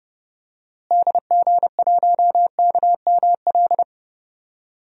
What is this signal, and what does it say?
Morse code, heard as a keyed tone, about 30 wpm, 700 Hz. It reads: DG1KML
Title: DXCC-100 Call Signs - 6 Characters - Repeated 3x - 30wpm